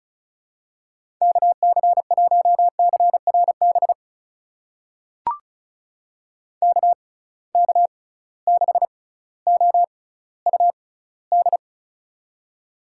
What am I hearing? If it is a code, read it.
Morse code, heard as a keyed tone, about 36 wpm, spaced as for 18 wpm, 700 Hz. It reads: KC1CRB KK6OUD
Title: USA Call Signs - 6 Characters - 6x Character Spacing - 35wpm